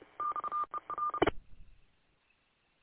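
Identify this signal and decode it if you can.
Morse code, heard as a keyed tone, about 30 wpm, 1200 Hz. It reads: XER
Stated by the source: Recorded at W8UM.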